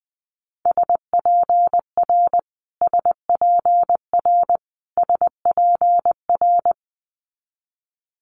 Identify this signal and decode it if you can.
Morse code, heard as a keyed tone, about 20 wpm, 700 Hz. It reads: SPR SPR SPR
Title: Multi-letter Phonemes Repeated 3x 20wpm